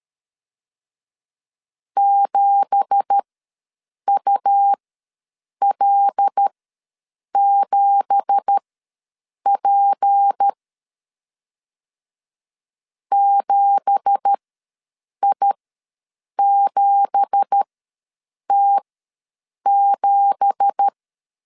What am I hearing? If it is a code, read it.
Morse code, heard as a keyed tone, about 13 wpm, spaced as for 7 wpm, 800 Hz. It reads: 7UL7P 7I7T7
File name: LEZIONI / 9 2867 ws 12wpm 3431